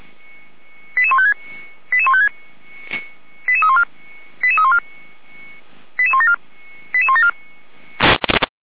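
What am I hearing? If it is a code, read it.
Five-tone selective call (ZVEI), heard as quick runs of five tones, tones of about 80 ms. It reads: 88156 88156 88314 88314 88174 88174
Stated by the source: ZVEI five-tone selective call received with an SDR.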